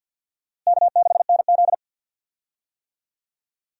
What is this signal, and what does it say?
Morse code, heard as a keyed tone, about 46 wpm, 700 Hz. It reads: K6NZ
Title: USA Call Signs - 4 Characters - Repeated 3x - 50wpm